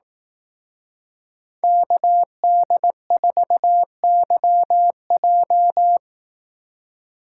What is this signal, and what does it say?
Morse code, heard as a keyed tone, about 18 wpm, 700 Hz. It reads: KD4YJ
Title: Speed Racing - USA 2x2 Call Signs 15wpm